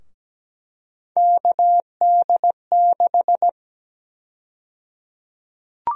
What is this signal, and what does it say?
Morse code, heard as a keyed tone, about 17 wpm, 700 Hz. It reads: KD6
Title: USA Call Sign Prefixes - Repeated 2x - 17wpm